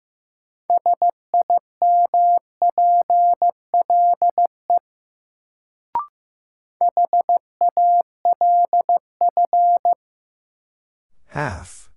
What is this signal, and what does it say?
Morse code, heard as a keyed tone, about 15 wpm, 700 Hz. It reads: SIMPLE HALF
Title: Sentences from Top 400 Words - New Word Drill - 15wpm